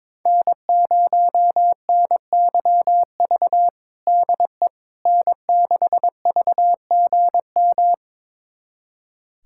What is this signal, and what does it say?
Morse code, heard as a keyed tone, about 22 wpm, 700 Hz. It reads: N0NYV DE N6VGM
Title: USA Callsign DE Callsign - 5 Characters - 22wpm